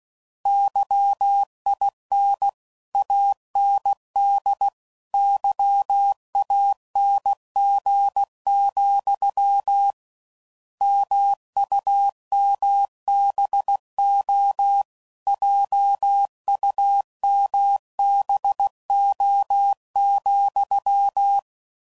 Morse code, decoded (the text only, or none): YIN AND YANG, MUMBO JUMBO,